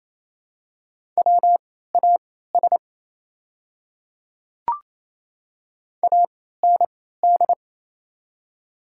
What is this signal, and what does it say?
Morse code, heard as a keyed tone, about 28 wpm, spaced as for 16 wpm, 700 Hz. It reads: WAS AND
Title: 3x Character Spacing - Top 100 Words - 28wpm